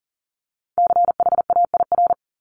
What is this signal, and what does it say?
Morse code, heard as a keyed tone, about 37 wpm, 700 Hz. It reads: CHAIR
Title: Multi-letter Phonemes Repeated 3x 40wpm